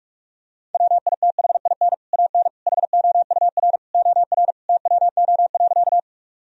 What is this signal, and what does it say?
Morse code, heard as a keyed tone, about 41 wpm, 700 Hz. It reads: WITHIN AN HOUR OR TWO.
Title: Sentences from Top 1000 Words - Review Part 4 - 45wpm